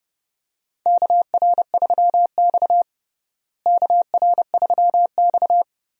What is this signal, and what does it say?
Morse code, heard as a keyed tone, about 31 wpm, 700 Hz. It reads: KR3X KR3X
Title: USA Call Signs - 4 Characters - Repeated 2x - 30wpm